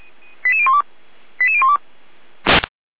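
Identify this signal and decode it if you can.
Five-tone selective call (ZVEI), heard as quick runs of five tones, tones of about 70 ms. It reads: 88012 88012
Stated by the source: ZVEI five-tone selective call 88124 received by SDR.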